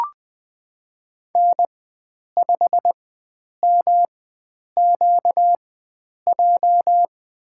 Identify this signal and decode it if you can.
Morse code, heard as a keyed tone, about 20 wpm, spaced as for 9 wpm, 700 Hz. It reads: N5MQJ
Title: USA Call Signs - 4x Character Spacing - 20wpm